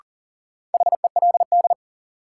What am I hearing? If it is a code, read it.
Morse code, heard as a keyed tone, about 40 wpm, 700 Hz. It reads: HELD
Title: Sentences from Top 1600 Words - New Word Drill - 40wpm